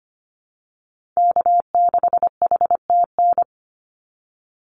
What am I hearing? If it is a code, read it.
Morse code, heard as a keyed tone, about 25 wpm, 700 Hz. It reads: K6HTN
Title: Call Signs - 25wpm